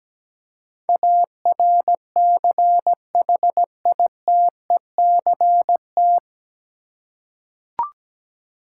Morse code, heard as a keyed tone, about 17 wpm, 700 Hz. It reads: ARCHITECT
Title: Practice QSO - Exchange 3 - More Info - 17wpm